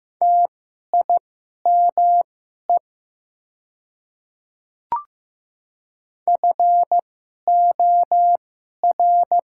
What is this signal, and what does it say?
Morse code, heard as a keyed tone, about 15 wpm, spaced as for 11 wpm, 700 Hz. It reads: TIME FOR
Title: 2x Character Spacing - Top 100 Words - 15wpm